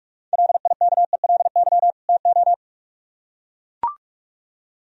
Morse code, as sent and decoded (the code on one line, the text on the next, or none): .-.. .. -.- . .-.. -.-- / - ---
LIKELY TO